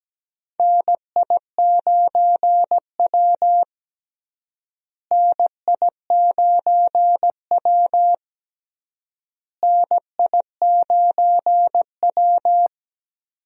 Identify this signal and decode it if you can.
Morse code, heard as a keyed tone, about 17 wpm, 700 Hz. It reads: NI9W NI9W NI9W
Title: USA Call Signs - 4 Characters - Repeated 3x - 17wpm